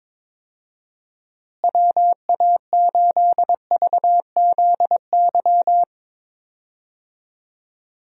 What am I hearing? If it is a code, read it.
Morse code, heard as a keyed tone, about 22 wpm, 700 Hz. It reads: WA8VZY
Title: USA Call Signs - 6 Characters - Repeated 3x - 22wpm